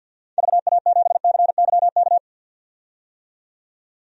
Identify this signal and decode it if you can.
Morse code, heard as a keyed tone, about 51 wpm, 700 Hz. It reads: VA7CYK
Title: DXCC-100 Call Signs - 6 Characters - 50wpm